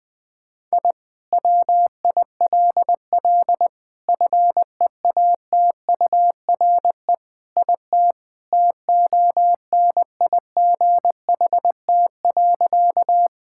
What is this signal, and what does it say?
Morse code, heard as a keyed tone, about 20 wpm, 700 Hz. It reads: I WILL FEATURE IT TONIGHT.
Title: Sentences from Top 1000 Words - Review Part 5 - 20wpm